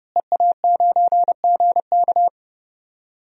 Morse code, heard as a keyed tone, about 30 wpm, 700 Hz. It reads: EA9GK
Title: DXCC-100 Call Signs - 5 Characters - Repeated 2x - 30wpm